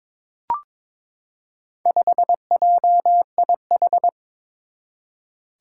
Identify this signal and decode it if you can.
Morse code, heard as a keyed tone, about 22 wpm, 700 Hz. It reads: 5JIH
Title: Four Challenging Letter-Number Combinations 22wpm